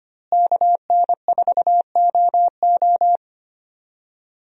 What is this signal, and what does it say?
Morse code, heard as a keyed tone, about 25 wpm, 700 Hz. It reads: KN4OO
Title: USA Call Signs - 5 Characters - Repeated 3x - 25wpm